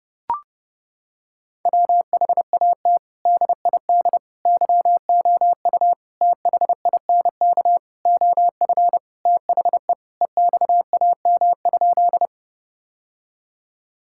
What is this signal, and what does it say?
Morse code, heard as a keyed tone, about 32 wpm, 700 Hz. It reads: WHAT DID YOU THINK OF THE EXAM?
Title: Sentences from Top 1000 Words - Review Part 5 - 30wpm